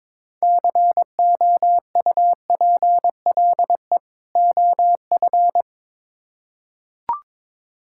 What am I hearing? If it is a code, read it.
Morse code, heard as a keyed tone, about 22 wpm, 700 Hz. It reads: COUPLE OF
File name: Build A Phrase - 2-Word Phrases - 22wpm